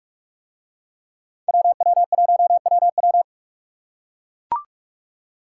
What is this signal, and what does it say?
Morse code, heard as a keyed tone, about 41 wpm, 700 Hz. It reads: WW1WW